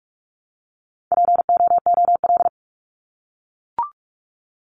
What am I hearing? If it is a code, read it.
Morse code, heard as a keyed tone, about 44 wpm, 700 Hz. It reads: POOL
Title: Top 1751-2000 Words 45wpm